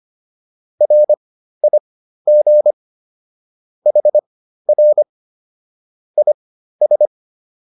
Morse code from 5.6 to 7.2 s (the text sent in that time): IS